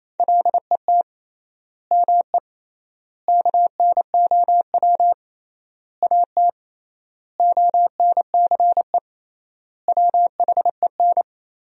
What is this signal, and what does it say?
Morse code, heard as a keyed tone, about 28 wpm, 700 Hz. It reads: LET ME KNOW AT ONCE WHEN
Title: 3x Word-Spacing - Sentences from Top 300 Words - 28wpm